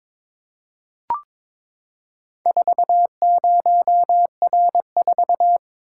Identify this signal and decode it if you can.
Morse code, heard as a keyed tone, about 22 wpm, 690 Hz. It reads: 40R4
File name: practice / 22wpm / Four Letter-Number Combinations - 22wpm